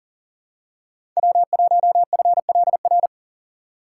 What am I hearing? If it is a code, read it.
Morse code, heard as a keyed tone, about 39 wpm, 700 Hz. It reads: W1FLR
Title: USA Call Signs - 5 Characters - 2x Character Spacing - 40wpm